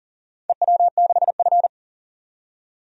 Morse code, heard as a keyed tone, about 38 wpm, 700 Hz. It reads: EW6F
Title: DXCC-100 Call Signs - 4 Characters - 40wpm